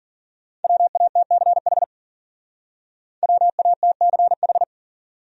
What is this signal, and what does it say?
Morse code, heard as a keyed tone, about 50 wpm, 700 Hz. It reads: WATCH WATCH